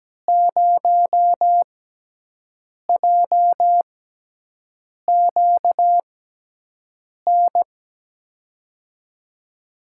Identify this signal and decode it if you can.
Morse code, heard as a keyed tone, about 17 wpm, spaced as for 6 wpm, 700 Hz. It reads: 0JQN
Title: USA Call Signs - 6x Character Spacing - 17wpm